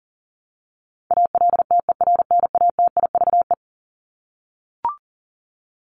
Morse code, heard as a keyed tone, about 37 wpm, 700 Hz. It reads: ALTERNATIVE
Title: Top 501-750 Words 40wpm